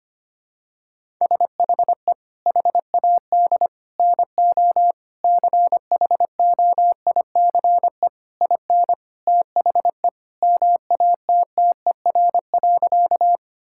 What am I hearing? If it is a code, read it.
Morse code, heard as a keyed tone, about 25 wpm, 700 Hz. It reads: SHE HAD NO CHOICE IN THE MATTER.
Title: Sentences from Top 1000 Words - Review Part 6 - 25wpm